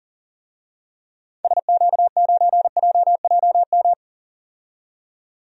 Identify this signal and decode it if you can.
Morse code, heard as a keyed tone, about 40 wpm, 700 Hz. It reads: SQ9JJM
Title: DXCC-100 Call Signs - 6 Characters - 40wpm